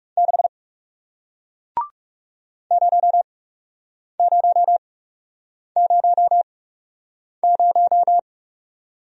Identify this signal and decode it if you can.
Morse code, heard as a keyed tone, about 37 wpm, spaced as for 8 wpm, 700 Hz. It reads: 6 0000
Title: Speed-Racing Single Letter-Number 30wpm